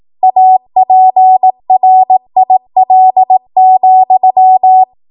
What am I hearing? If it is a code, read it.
Morse code, heard as a keyed tone, about 18 wpm, 750 Hz. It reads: APRIL,
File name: Morse / 18WPM